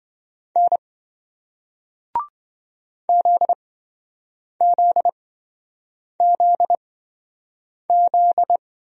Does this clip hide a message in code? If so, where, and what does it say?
Yes: Morse code from 0.6 to 8.6 s, heard as a keyed tone, about 27 wpm, spaced as for 7 wpm, 700 Hz. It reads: N ZZZZ